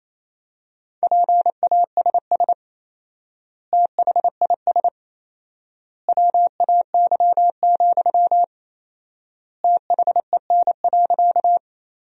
Morse code, heard as a keyed tone, about 28 wpm, 700 Hz. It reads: PASS THIS WAY, THEN.
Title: 4x Word-Spacing - Sentences from Top 400 Words - 28wpm